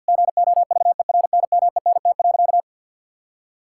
Morse code, heard as a keyed tone, about 50 wpm, 700 Hz. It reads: GOVERNMENT.
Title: Sentences from Top 1000 Words - Review Part 6 - 50wpm